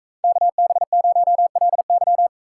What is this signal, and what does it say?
Morse code, heard as a keyed tone, about 42 wpm, 700 Hz. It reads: KB0LY